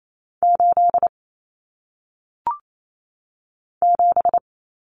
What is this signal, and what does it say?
Morse code, heard as a keyed tone, about 28 wpm, 700 Hz. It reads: 8 7
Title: Common Cut Numbers - 1-Digit 28wpm